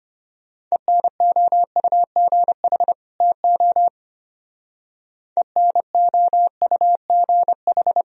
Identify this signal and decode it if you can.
Morse code, heard as a keyed tone, about 31 wpm, 700 Hz. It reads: ENOUGH TO ENOUGH
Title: Speed-Racing - Sets of 2 Words 25wpm